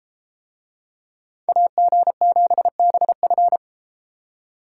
Morse code, heard as a keyed tone, about 33 wpm, 700 Hz. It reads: AG7BF